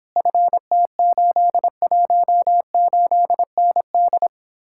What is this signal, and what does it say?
Morse code, heard as a keyed tone, about 26 wpm, 700 Hz. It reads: FT818ND